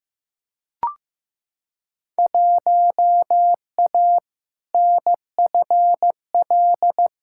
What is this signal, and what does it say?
Morse code, heard as a keyed tone, about 15 wpm, 700 Hz. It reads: 1A NFL
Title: Field Day Exchanges - 15wpm